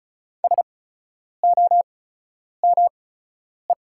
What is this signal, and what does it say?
Morse code, heard as a keyed tone, about 36 wpm, spaced as for 10 wpm, 700 Hz. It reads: SOME
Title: 8x Character Spacing - Top 100 Words - 35wpm